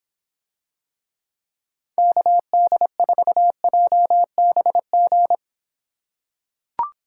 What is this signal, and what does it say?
Morse code, heard as a keyed tone, about 26 wpm, 700 Hz. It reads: KD4JBG